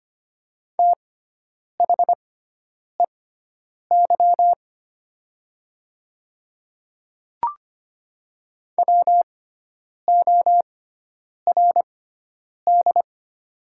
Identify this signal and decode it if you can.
Morse code, heard as a keyed tone, about 25 wpm, spaced as for 9 wpm, 700 Hz. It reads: THEY WORD
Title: 6x Character Spacing - Top 100 Words - 25wpm